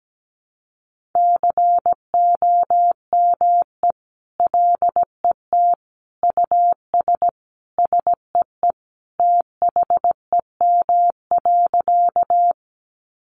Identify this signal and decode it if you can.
Morse code, heard as a keyed tone, about 17 wpm, 700 Hz. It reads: COME LET US SEE THEM.
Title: Sentences from Top 300 Words 17wpm